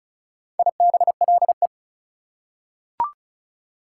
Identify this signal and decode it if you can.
Morse code, heard as a keyed tone, about 36 wpm, 700 Hz. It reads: IBLE
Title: Prefixes and Suffixes 35wpm